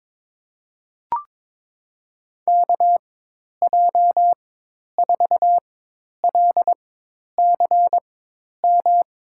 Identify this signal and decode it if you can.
Morse code, heard as a keyed tone, about 22 wpm, spaced as for 10 wpm, 700 Hz. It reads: KJ4LCM
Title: USA Call Signs - 6 Characters - 4x Character Spacing - 22wpm